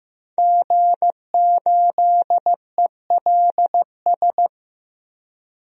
Morse code, heard as a keyed tone, about 15 wpm, 700 Hz. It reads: G8ELS
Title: DXCC-100 Call Signs - 5 Characters - Repeated 2x - 15wpm